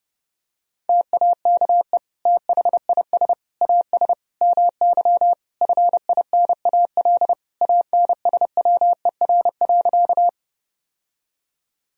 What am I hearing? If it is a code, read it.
Morse code, heard as a keyed tone, about 30 wpm, 700 Hz. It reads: TAKE THIS AS MY FINAL ANSWER.
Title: Sentences from Top 1000 Words - Review Part 2 - 30wpm